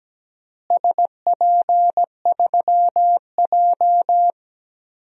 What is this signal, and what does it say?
Morse code, heard as a keyed tone, about 17 wpm, 700 Hz. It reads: SP3J
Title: DXCC-100 Call Signs - 4 Characters - Repeated 3x - 17wpm